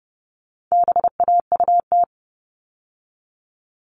Morse code, heard as a keyed tone, about 30 wpm, 700 Hz. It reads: BAUT
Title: All Cut Numbers - 4-Digit 30wpm